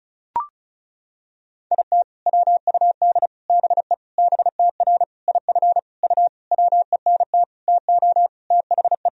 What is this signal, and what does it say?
Morse code, heard as a keyed tone, about 34 wpm, 700 Hz. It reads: IT WUD BE BTR IF U WENT TO THE
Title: Sentences from Top 400 Words and Abbreviations 35wpm